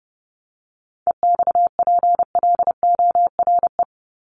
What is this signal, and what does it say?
Morse code, heard as a keyed tone, about 30 wpm, 700 Hz. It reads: EXPLORE
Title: Top 1001-1250 Words 30wpm